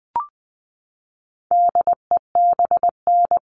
Morse code, heard as a keyed tone, about 20 wpm, 700 Hz. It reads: DEBN
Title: All Cut Numbers - 4-Digit 20wpm